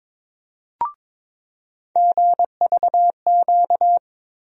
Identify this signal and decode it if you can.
Morse code, heard as a keyed tone, about 22 wpm, 700 Hz. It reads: GVQ
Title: USA Call Sign Postfixes - 3 Letters 22wpm